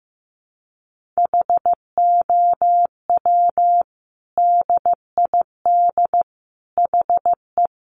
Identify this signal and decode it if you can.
Morse code, heard as a keyed tone, about 15 wpm, 700 Hz. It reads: HOW DID HE
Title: Sentences from Top 1500 Words 15wpm